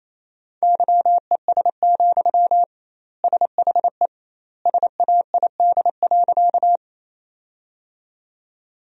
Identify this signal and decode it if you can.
Morse code, heard as a keyed tone, about 28 wpm, 700 Hz. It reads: YES, SHE SAID.
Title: 2x Word-Spacing - Sentences from Top 500 Words - 28wpm